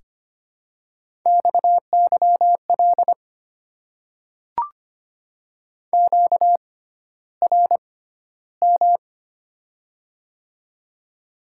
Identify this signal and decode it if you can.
Morse code, heard as a keyed tone, about 25 wpm, spaced as for 13 wpm, 700 Hz. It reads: XYL QRM
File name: Top 90 QSO Elements - 6x Character Spacing - 25wpm